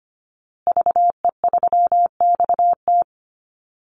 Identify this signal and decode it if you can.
Morse code, heard as a keyed tone, about 25 wpm, 700 Hz. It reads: VE3XT
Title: Call Signs Repeated 2x - 25wpm